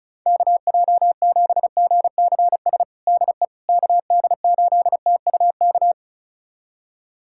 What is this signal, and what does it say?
Morse code, heard as a keyed tone, about 36 wpm, 700 Hz. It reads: KJ7GCS DE KD8TUK
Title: USA Callsign DE Callsign 35wpm